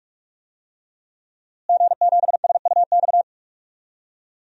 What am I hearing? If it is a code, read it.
Morse code, heard as a keyed tone, about 42 wpm, 700 Hz. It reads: G7SUX
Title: DXCC-100 Call Signs - 5 Characters - Repeated 2x - 45wpm